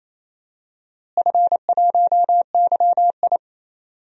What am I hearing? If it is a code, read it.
Morse code, heard as a keyed tone, about 28 wpm, 700 Hz. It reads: F1YI
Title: DXCC-100 Call Signs - 4 Characters - Repeated 3x - 28wpm